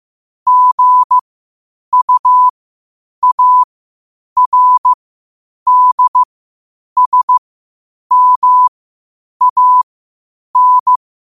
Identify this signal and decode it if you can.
Morse code, heard as a keyed tone, about 15 wpm, spaced as for 8 wpm, 1000 Hz. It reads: GUARDSMAN
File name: qotdfiles / 8wpm